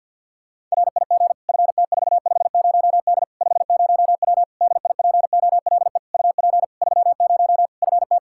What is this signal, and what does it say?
Morse code, heard as a keyed tone, about 50 wpm, 700 Hz. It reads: RIG FT450D 50W DIPOLE UP 30 FT
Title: Practice QSO - Exchange 2 - Info - 50wpm